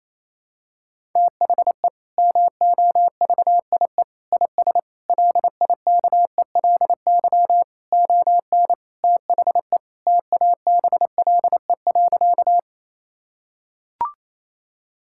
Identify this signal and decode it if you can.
Morse code, heard as a keyed tone, about 27 wpm, 700 Hz. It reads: THE MOVIE IS LIKELY ON THE TABLE.
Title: Sentences from Top 1000 Words - Review Part 5 - 28wpm